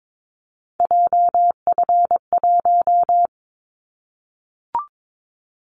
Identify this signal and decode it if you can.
Morse code, heard as a keyed tone, about 22 wpm, 700 Hz. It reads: JF1
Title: Callsign Prefixes for Top 25 DXCC - Repeated 3x - 22wpm